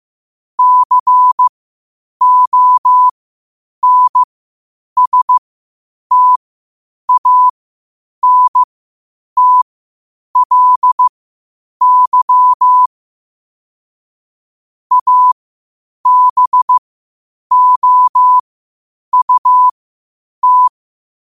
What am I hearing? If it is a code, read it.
Morse code, heard as a keyed tone, about 15 wpm, spaced as for 8 wpm, 1000 Hz. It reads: CONSTANTLY ABOUT
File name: qotdfiles / 8wpm